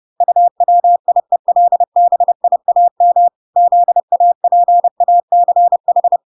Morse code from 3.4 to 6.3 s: ZAPACH